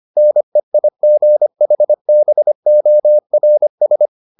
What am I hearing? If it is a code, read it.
Morse code, heard as a keyed tone, about 25 wpm, 600 Hz. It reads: NEIGHBORS